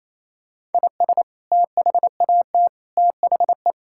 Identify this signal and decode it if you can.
Morse code, heard as a keyed tone, about 28 wpm, 700 Hz. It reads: IS THAT THE
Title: Sentences from Top 1000 Words - Review Part 5 - 28wpm